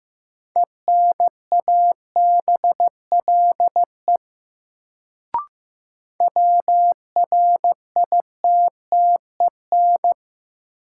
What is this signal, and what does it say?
Morse code, heard as a keyed tone, about 15 wpm, 700 Hz. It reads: ENABLE WRITTEN
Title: Sentences from Top 1500 Words - New Word Drill Alt - 15wpm